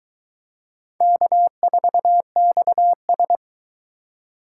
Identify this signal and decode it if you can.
Morse code, heard as a keyed tone, about 21 wpm, 700 Hz. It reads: K4XS